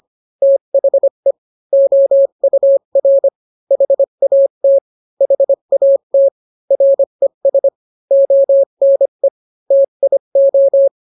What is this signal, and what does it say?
Morse code, heard as a keyed tone, about 25 wpm, 550 Hz. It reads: THE OUR HAT HAT RES ONE TIO